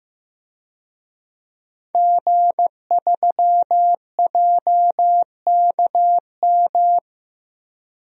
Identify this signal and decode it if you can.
Morse code, heard as a keyed tone, about 15 wpm, 700 Hz. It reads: G3JKM
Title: DXCC-100 Call Signs - 5 Characters - Repeated 2x - 15wpm